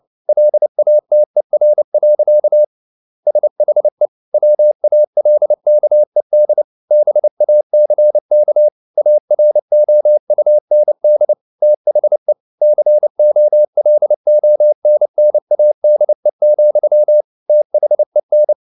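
Morse code, heard as a keyed tone, about 29 wpm, 600 Hz. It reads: LATER. SHE WALKED BACK AROUND THE COLONNADE, THEN
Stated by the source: Morse code at 29 wpm, sending Ken Follett's novel Never in English, from CW med Gnister.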